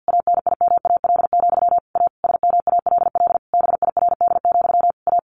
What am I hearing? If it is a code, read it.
Morse code, heard as a keyed tone, about 50 wpm, 700 Hz. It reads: ANIMAL, A SMALL BIRD, A
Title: A Sound Of Thunder 50wpm